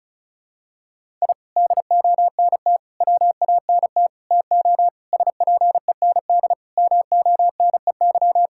Morse code, heard as a keyed tone, about 36 wpm, 700 Hz. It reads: I DONT WANT TO SPEND MONEY
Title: Sentences from Top 1000 Words - Review Part 5 - 35wpm